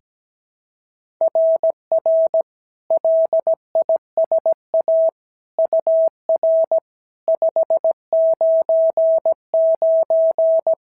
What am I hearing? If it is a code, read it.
Morse code, heard as a keyed tone, about 16 wpm, 650 Hz. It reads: RR LISA UR 599
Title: SOTA QSO Practice 17wpm